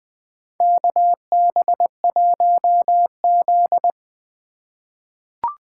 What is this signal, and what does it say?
Morse code, heard as a keyed tone, about 20 wpm, 700 Hz. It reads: KB1Z
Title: USA Call Signs - 4 Characters - 2x Character Spacing - 20wpm